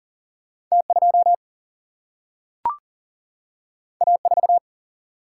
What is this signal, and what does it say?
Morse code, heard as a keyed tone, about 40 wpm, 700 Hz. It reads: T2 A4
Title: Single-Letter Single-Number 40wpm